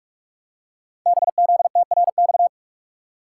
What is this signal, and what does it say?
Morse code, heard as a keyed tone, about 44 wpm, 700 Hz. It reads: BZTRX
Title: Code Groups - 5 Letter:Numbers - 45wpm